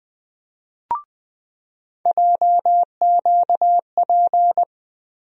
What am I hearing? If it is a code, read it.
Morse code, heard as a keyed tone, about 21 wpm, 700 Hz. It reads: JQP
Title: USA Call Sign Postfixes - 3 Letters 20wpm